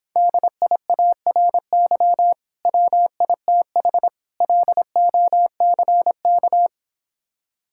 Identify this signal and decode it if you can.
Morse code, heard as a keyed tone, about 26 wpm, 700 Hz. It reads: DIARY WITH LOCK